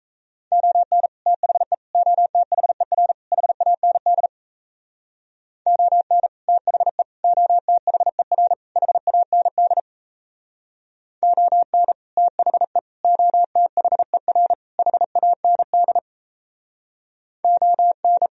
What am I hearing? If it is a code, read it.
Morse code, heard as a keyed tone, about 35 wpm, 700 Hz. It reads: ON THE OTHER HAND ON THE OTHER HAND ON THE OTHER HAND ON